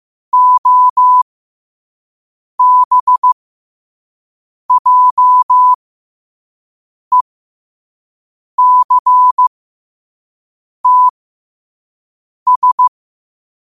Morse code, heard as a keyed tone, about 15 wpm, spaced as for 5 wpm, 1000 Hz. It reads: OBJECTS